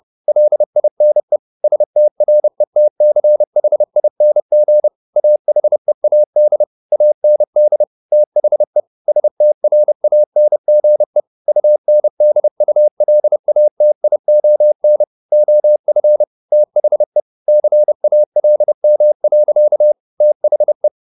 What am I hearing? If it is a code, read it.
Morse code, heard as a keyed tone, about 30 wpm, 600 Hz. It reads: LINE STRETCHING AHEAD AND THE STRANGE UNDULATION OF THE CALM. THE
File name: the old man and the sea / the old man and the sea 1 30wpm